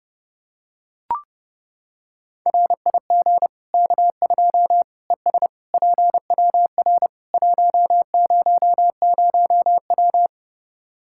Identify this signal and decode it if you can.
Morse code, heard as a keyed tone, about 32 wpm, 700 Hz. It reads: RIG K2 ES PWR 100W
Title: K1USN SST QSO Practice 30wpm